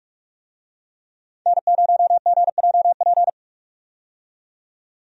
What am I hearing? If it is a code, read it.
Morse code, heard as a keyed tone, about 46 wpm, 700 Hz. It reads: N0GJP
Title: USA Call Signs - 5 Characters - 45wpm